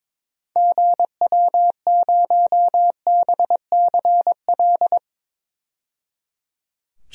Morse code, heard as a keyed tone, about 22 wpm, 700 Hz. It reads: GW0BCL
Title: DXCC-100 Call Signs - 6 Characters - 22wpm